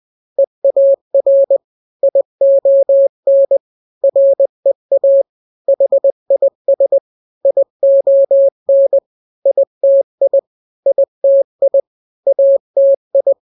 Morse code, heard as a keyed tone, about 20 wpm, 550 Hz. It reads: EAR ION REA HIS ION ITI ITI ATI
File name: n-grams / 3-grams 20 18 WPM